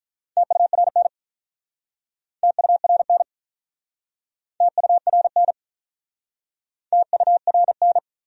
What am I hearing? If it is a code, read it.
Morse code, heard as a keyed tone, about 47 wpm, 700 Hz. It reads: TURN TURN TURN TURN